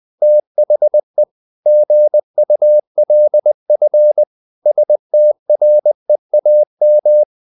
Morse code, heard as a keyed tone, about 20 wpm, 600 Hz. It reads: THE GULF STREAM
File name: the old man and the sea / the old man and the sea part 1 20wpm